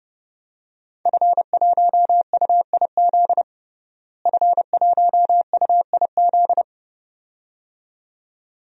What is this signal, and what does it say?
Morse code, heard as a keyed tone, about 32 wpm, 700 Hz. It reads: F1UIZ F1UIZ
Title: DXCC-100 Call Signs - 5 Characters - Repeated 2x - 30wpm